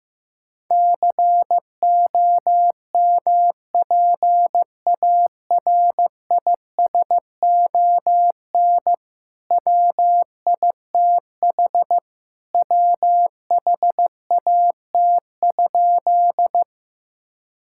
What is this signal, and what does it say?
Morse code, heard as a keyed tone, about 15 wpm, 700 Hz. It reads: COMPARISON WITH WHAT?
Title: Sentences from Top 1600 Words 15wpm